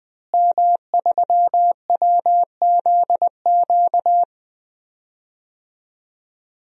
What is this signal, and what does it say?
Morse code, heard as a keyed tone, about 20 wpm, 700 Hz. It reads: M3WZQ
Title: DXCC-100 Call Signs - 5 Characters - Repeated 2x - 20wpm